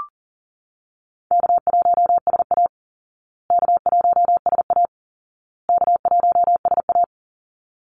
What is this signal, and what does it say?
Morse code, heard as a keyed tone, about 40 wpm, 700 Hz. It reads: K1SA K1SA K1SA
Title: Easy Call Signs Repeated 3x - 40wpm